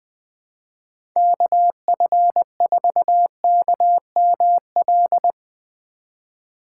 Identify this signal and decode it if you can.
Morse code, heard as a keyed tone, about 20 wpm, 700 Hz. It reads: KF4KML
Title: Speed Racing - USA 2x3 Call Signs 17wpm